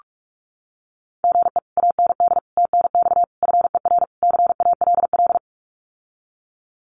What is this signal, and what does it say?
Morse code, heard as a keyed tone, about 46 wpm, 700 Hz. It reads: GE AND TNX FER CALL
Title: Common QSO Phrases Encore 45wpm